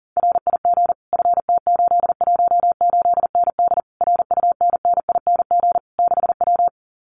Morse code, heard as a keyed tone, about 40 wpm, 700 Hz. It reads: RIG FT818ND RUNNING 6W
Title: Common QSO Phrases Encore 40wpm